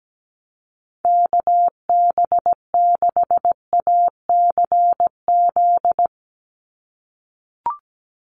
Morse code, heard as a keyed tone, about 17 wpm, 700 Hz. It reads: KB6ACZ
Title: USA Call Signs 17wpm